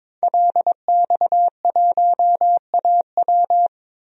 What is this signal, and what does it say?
Morse code, heard as a keyed tone, about 22 wpm, 700 Hz. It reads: LX1AW
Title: DXCC-100 Call Signs - 5 Characters - Repeated 3x - 22wpm